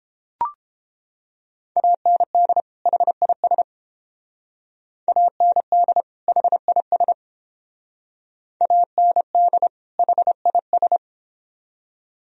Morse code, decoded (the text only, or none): AND HIS AND HIS AND HIS